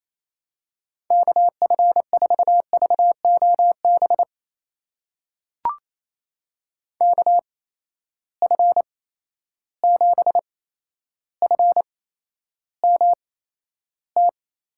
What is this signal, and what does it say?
Morse code, heard as a keyed tone, about 28 wpm, spaced as for 12 wpm, 700 Hz. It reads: KF4VOB KF7FMT
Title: USA Call Signs - 6 Characters - 8x Character Spacing - 28wpm